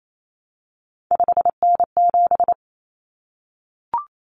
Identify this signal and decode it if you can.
Morse code, heard as a keyed tone, about 28 wpm, 700 Hz. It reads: 5N7
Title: Common Cut Numbers - 3-Digit 28wpm